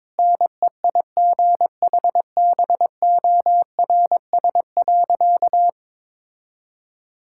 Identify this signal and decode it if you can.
Morse code, heard as a keyed tone, about 22 wpm, 700 Hz. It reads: NEIGHBORS.
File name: practice / 22wpm / Sentences with 4-Word Phrases - 22wpm